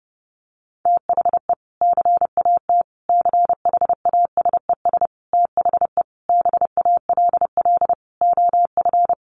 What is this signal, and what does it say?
Morse code, heard as a keyed tone, about 30 wpm, 700 Hz. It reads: THE CAT CHASES THE BALL OF
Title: Additional Easy Phrases 30wpm